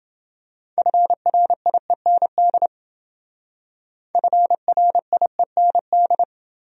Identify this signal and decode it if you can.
Morse code, heard as a keyed tone, about 28 wpm, 700 Hz. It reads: FRIEND FRIEND